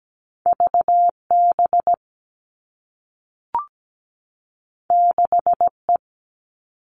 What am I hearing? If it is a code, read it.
Morse code, heard as a keyed tone, about 17 wpm, 700 Hz. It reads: VB 6E
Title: All Cut Numbers - 2-Digit 17wpm